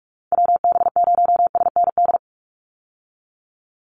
Morse code, heard as a keyed tone, about 44 wpm, 700 Hz. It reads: WB0SND
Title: Call Signs Repeated 3x - 45wpm